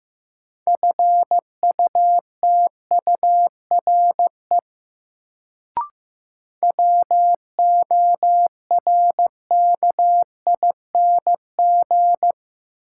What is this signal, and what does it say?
Morse code, heard as a keyed tone, about 15 wpm, 700 Hz. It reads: FUTURE WORKING
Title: Sentences from Top 700 Words - New Word Drill - 15wpm